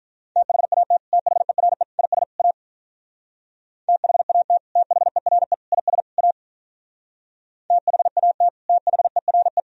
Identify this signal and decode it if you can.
Morse code, heard as a keyed tone, about 43 wpm, 700 Hz. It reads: THAT THERE IS A THAT THERE IS A THAT THERE
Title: Speed-Racing - Sets of 4 Words 35wpm